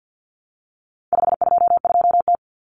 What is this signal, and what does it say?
Morse code, heard as a keyed tone, about 48 wpm, 700 Hz. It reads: 522T
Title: Common Cut Numbers - 4-Digit 50wpm